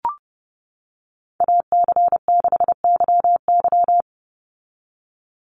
Morse code, heard as a keyed tone, about 30 wpm, 700 Hz. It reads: AC6YY
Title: Call Signs - 30wpm